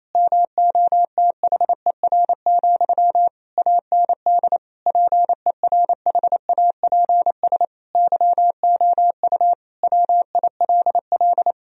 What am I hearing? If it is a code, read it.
Morse code, heard as a keyed tone, about 28 wpm, 700 Hz. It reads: MOTHER, AND PERHAPS YOU WILL
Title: Sentences from Top 1000 Words - Review Part 1 - 28wpm